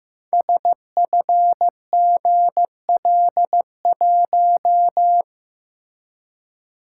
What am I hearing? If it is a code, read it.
Morse code, heard as a keyed tone, about 15 wpm, 700 Hz. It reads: SFGL1
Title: Code Groups - 5 Letter:Numbers - 15wpm